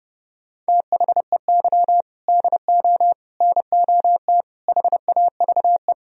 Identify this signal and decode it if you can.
Morse code, heard as a keyed tone, about 30 wpm, 700 Hz. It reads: THEY DO NOT HAVE